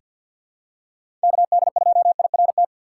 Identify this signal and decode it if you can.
Morse code, heard as a keyed tone, about 50 wpm, 700 Hz. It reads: KD2IRT
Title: USA Call Signs - 4x Character Spacing - 50wpm